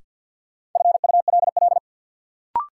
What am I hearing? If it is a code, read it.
Morse code, heard as a keyed tone, about 53 wpm, 700 Hz. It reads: FULL
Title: Sentences from Top 500 Words - New Word Drill - 50wpm